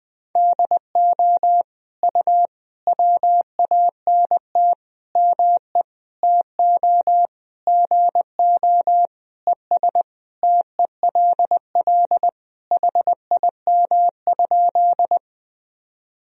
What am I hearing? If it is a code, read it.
Morse code, heard as a keyed tone, about 20 wpm, 700 Hz. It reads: DO U WANT ME TO GO ES TELL HIM?
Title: Sentences from Top 200 Words and Abbreviations 20wpm